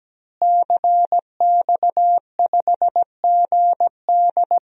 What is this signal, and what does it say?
Morse code, heard as a keyed tone, about 17 wpm, 700 Hz. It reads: CX5GD
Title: DXCC-100 Call Signs - 5 Characters - Repeated 2x - 17wpm